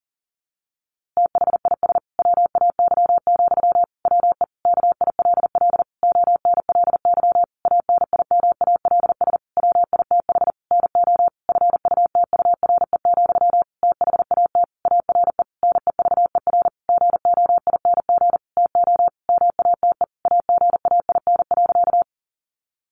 Morse code, heard as a keyed tone, about 40 wpm, 700 Hz. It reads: THIS WAY, WE KILL ONLY ANIMALS WITH NO FUTURE, THAT ARE NEVER GOING TO MATE AGAIN.
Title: A Sound Of Thunder 40wpm